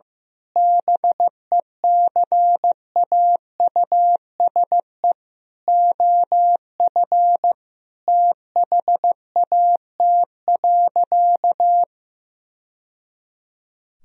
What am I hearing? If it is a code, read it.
Morse code, heard as a keyed tone, about 15 wpm, 700 Hz. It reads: BECAUSE OF THAT.
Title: Sentences from Top 1000 Words - Review Part 2 - 15wpm